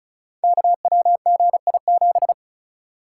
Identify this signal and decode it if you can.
Morse code, heard as a keyed tone, about 36 wpm, 700 Hz. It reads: KWGI7
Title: Code Groups - 5 Letter:Numbers - 35wpm